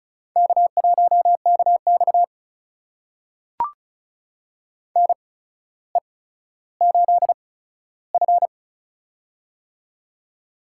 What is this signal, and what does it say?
Morse code, heard as a keyed tone, about 36 wpm, spaced as for 15 wpm, 700 Hz. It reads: K1KX NE8F
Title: USA Call Signs - 4 Characters - 8x Character Spacing - 35wpm